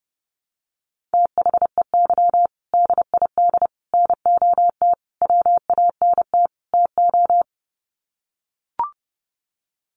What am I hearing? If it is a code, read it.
Morse code, heard as a keyed tone, about 30 wpm, 700 Hz. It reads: THEY DID NOT WANT TO
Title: Sets of 5 Words 30wpm